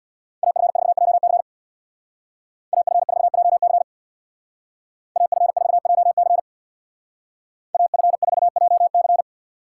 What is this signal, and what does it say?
Morse code, heard as a keyed tone, about 64 wpm, 700 Hz. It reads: AF4JC AF4JC AF4JC AF4JC